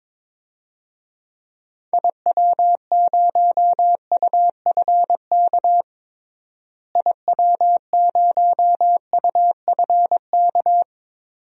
Morse code, heard as a keyed tone, about 22 wpm, 700 Hz. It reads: IW0UFK IW0UFK
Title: DXCC-100 Call Signs - 6 Characters - Repeated 2x - 22wpm